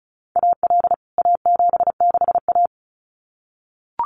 Morse code, heard as a keyed tone, about 34 wpm, 700 Hz. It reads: AL A76A
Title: CWT Exchange 35wpm